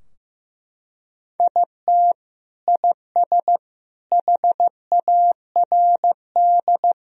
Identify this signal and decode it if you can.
Morse code, heard as a keyed tone, about 15 wpm, 700 Hz. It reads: IT IS HARD
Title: Sentences from Top 1000 Words - Review Part 4 - 15wpm